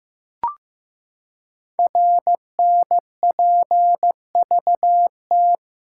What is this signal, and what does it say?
Morse code, heard as a keyed tone, about 15 wpm, 700 Hz. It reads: RNPVT
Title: Code Groups - 5 Letters - 15wpm